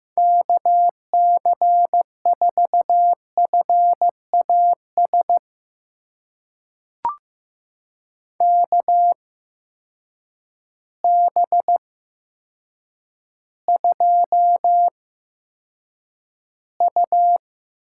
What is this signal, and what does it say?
Morse code, heard as a keyed tone, about 15 wpm, spaced as for 8 wpm, 700 Hz. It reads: KC4FAS KB2U
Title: USA Call Signs - 6 Characters - 8x Character Spacing - 15wpm